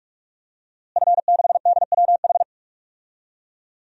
Morse code, heard as a keyed tone, about 45 wpm, 700 Hz. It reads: F6DWH
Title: DXCC-100 Call Signs - 5 Characters - Repeated 2x - 45wpm